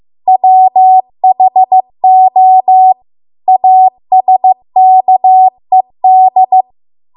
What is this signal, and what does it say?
Morse code, heard as a keyed tone, about 15 wpm, 750 Hz. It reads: WHO ASKED